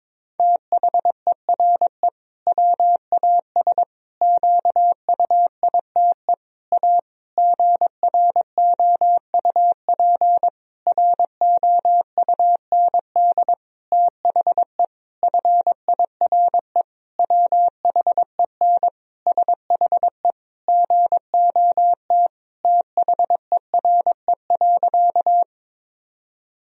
Morse code, heard as a keyed tone, about 22 wpm, 700 Hz. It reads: THERE WAS QUITE A GROUP ROUND THE FIRE WHEN SHE GOT THERE.
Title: Sentences from Top 1000 Words - Review Part 1 - 22wpm